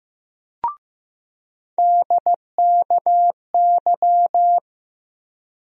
Morse code, heard as a keyed tone, about 16 wpm, 700 Hz. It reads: DKY
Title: USA Call Sign Postfixes - 3 Letters 15wpm